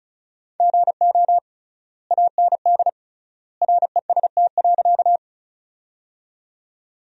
Morse code, heard as a keyed tone, about 34 wpm, 700 Hz. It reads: GO AND REST.
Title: 3x Word-Spacing - Sentences from Top 500 Words - 35wpm